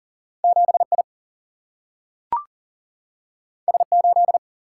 Morse code, heard as a keyed tone, about 40 wpm, 700 Hz. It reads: 7I S8
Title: Single-Letter Single-Number 40wpm